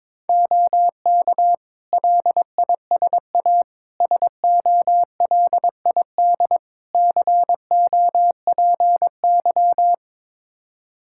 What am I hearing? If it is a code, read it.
Morse code, heard as a keyed tone, about 22 wpm, 700 Hz. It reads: OK LISA SOLID COPY
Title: Practice QSO - Exchange 3 - More Info - 22wpm